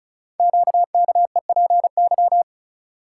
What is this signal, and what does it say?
Morse code, heard as a keyed tone, about 36 wpm, 700 Hz. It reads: QKEPY
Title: Code Groups - 5 Letters - 35wpm